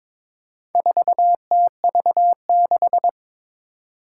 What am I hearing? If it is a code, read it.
Morse code, heard as a keyed tone, about 22 wpm, 700 Hz. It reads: 4TV6
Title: Four Challenging Letter-Number Combinations 22wpm